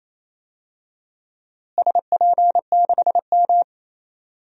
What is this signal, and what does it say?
Morse code, heard as a keyed tone, about 28 wpm, 700 Hz. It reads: SP6M
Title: DXCC-100 Call Signs - 4 Characters - Repeated 2x - 28wpm